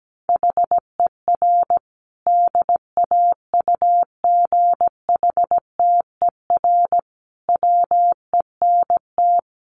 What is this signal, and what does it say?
Morse code, heard as a keyed tone, about 17 wpm, 700 Hz. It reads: HER DAUGHTER WENT
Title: Sentences from Top 1500 Words 17wpm